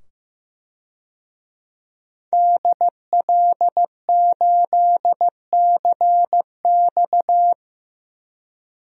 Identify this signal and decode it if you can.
Morse code, heard as a keyed tone, about 15 wpm, 700 Hz. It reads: DL8CX
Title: DXCC-100 Call Signs - 5 Characters - 15wpm